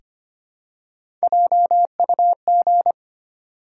Morse code, heard as a keyed tone, about 25 wpm, 700 Hz. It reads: JUG
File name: practice / 25wpm / Easy Words with Hints - 25wpm